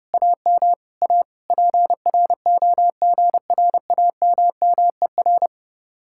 Morse code, heard as a keyed tone, about 37 wpm, 700 Hz. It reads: AM A PROGRAMMER
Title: Speed-Racing Common QSO Phrases - 30wpm